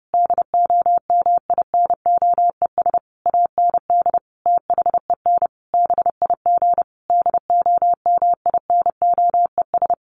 Morse code, heard as a keyed tone, about 30 wpm, 700 Hz. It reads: DOMINOES AND THEN BIG DOMINOES
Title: A Sound Of Thunder 30wpm